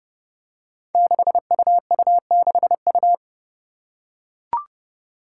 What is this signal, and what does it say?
Morse code, heard as a keyed tone, about 30 wpm, 700 Hz. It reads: 6UU6U
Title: Five Challenging Letter-Number Combinations 30wpm